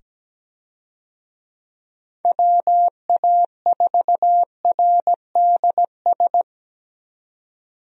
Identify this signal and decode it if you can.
Morse code, heard as a keyed tone, about 17 wpm, 700 Hz. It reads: WA4RDS
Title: USA Call Signs - 6 Characters - 17wpm